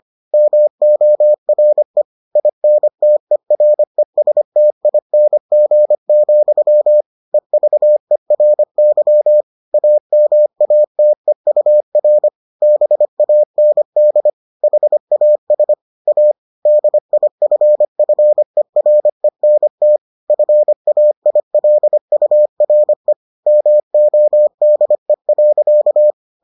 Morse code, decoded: MORE INTERESTING, EVERY AMATEUR BAND HAS A DIFFERENT FAILURE MODE.

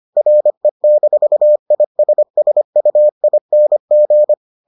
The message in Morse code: RE-ISSUING

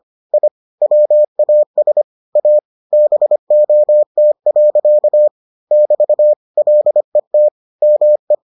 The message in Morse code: I WAS A BOT. <BT> LET ME

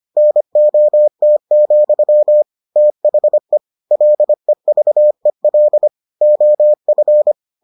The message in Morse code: NOT, THE LEVEL OF

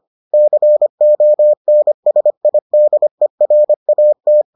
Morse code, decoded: CONSIDERAT

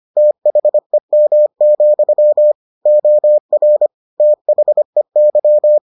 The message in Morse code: THEM, OR THEY

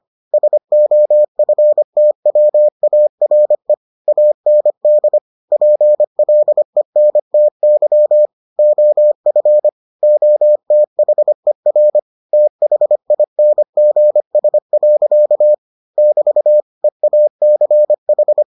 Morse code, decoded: SOFTWARE AND PLENTY OF OTHER THINGS. <BT> EACH

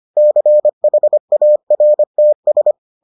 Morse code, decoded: CHARTS

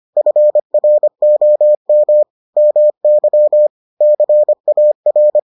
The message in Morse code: FROM MY CAR

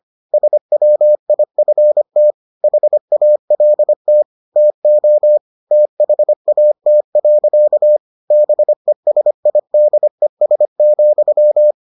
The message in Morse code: SWIFT HALT TO THAT. BESIDES,

